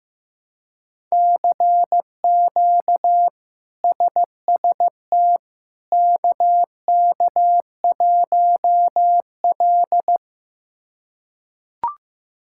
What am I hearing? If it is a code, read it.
Morse code, heard as a keyed tone, about 15 wpm, 700 Hz. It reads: CQ SST KK1L
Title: K1USN SST QSO Practice - 15wpm